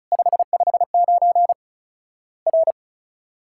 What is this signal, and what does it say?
Morse code, heard as a keyed tone, about 34 wpm, 700 Hz. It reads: 559 R
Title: Worked All Europe DX Contest - QTC Exchange - 35wpm